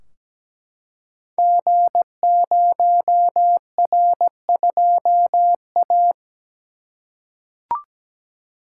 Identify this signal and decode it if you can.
Morse code, heard as a keyed tone, about 17 wpm, 700 Hz. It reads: G0R2A
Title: Code Groups - 5 Letter:Numbers - 17wpm